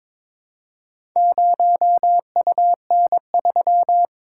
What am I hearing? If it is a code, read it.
Morse code, heard as a keyed tone, about 21 wpm, 700 Hz. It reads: 0UN3
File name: practice / 22wpm / Four Letter-Number Combinations - 22wpm